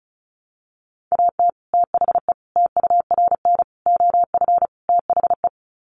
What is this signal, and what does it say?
Morse code, heard as a keyed tone, about 34 wpm, 700 Hz. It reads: AT THE TURN OF THE
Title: Sets of 5 Words 35wpm